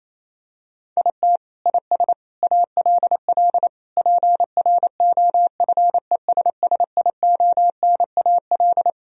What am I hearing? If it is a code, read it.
Morse code, heard as a keyed tone, about 28 wpm, 700 Hz. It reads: IT IS ALL PROFESSIONAL
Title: Sentences from Top 1000 Words - Review Part 4 - 28wpm